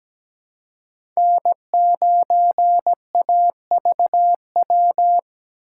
Morse code, heard as a keyed tone, about 17 wpm, 700 Hz. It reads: N9AVW
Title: USA Call Signs - 5 Characters - 2x Character Spacing - 17wpm